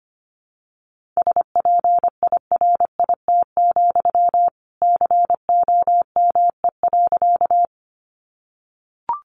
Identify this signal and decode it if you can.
Morse code, heard as a keyed tone, about 25 wpm, 700 Hz. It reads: SPIRIT, COME.
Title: Sentences from Top 1400 Words 25wpm